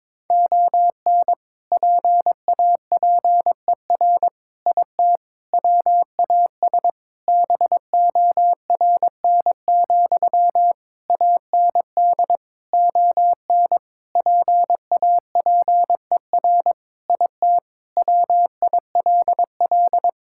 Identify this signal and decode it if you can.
Morse code, heard as a keyed tone, about 22 wpm, 700 Hz. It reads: ON PAPER IT WAS BORN, AND ON PAPER IT WILL
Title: Sentences from Top 1600 Words 22wpm